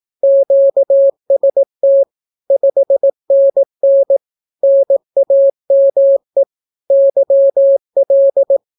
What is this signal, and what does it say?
Morse code, heard as a keyed tone, about 18 wpm, 550 Hz. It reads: QST 5NN NAME YL
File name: naukapodstaw / skroty / skroty